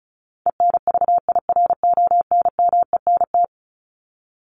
Morse code, heard as a keyed tone, about 34 wpm, 700 Hz. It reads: ENVIRONMENT
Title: Top 501-750 Words 35wpm